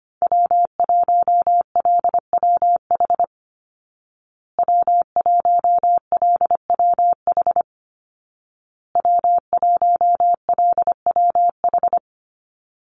Morse code, heard as a keyed tone, about 25 wpm, 700 Hz. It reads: W1LWH W1LWH W1LWH
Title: Call Signs Repeated 2x - 25wpm